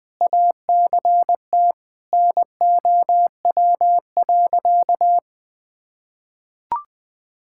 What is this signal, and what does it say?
Morse code, heard as a keyed tone, about 19 wpm, 700 Hz. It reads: ACT NOW.
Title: Sentences with 8-Word Phrases - Spoken Beginning - 20wpm